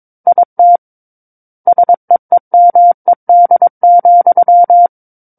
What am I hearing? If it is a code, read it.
Morse code, heard as a keyed tone, about 22 wpm, 700 Hz. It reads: IT SEEMED,
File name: Book One / 13-Chapter 13 - 22 WPM